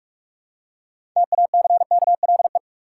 Morse code, heard as a keyed tone, about 46 wpm, 700 Hz. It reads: TACKLE
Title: Six Letter Words - 45wpm